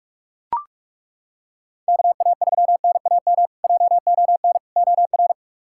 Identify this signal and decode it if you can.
Morse code, heard as a keyed tone, about 42 wpm, 700 Hz. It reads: KA3NAM JON OR
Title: K1USN SST QSO Practice - 45wpm